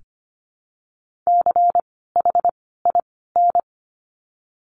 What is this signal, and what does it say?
Morse code, heard as a keyed tone, about 25 wpm, 700 Hz. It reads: C H I N
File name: course / 25:10wpm / 094 Lesson 16f - Letter c - Words 25wpm